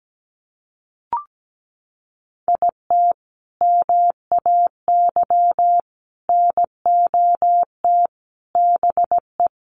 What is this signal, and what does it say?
Morse code, heard as a keyed tone, about 17 wpm, 700 Hz. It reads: IT MAY NOT BE
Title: Sentences from Top 1400 Words 17wpm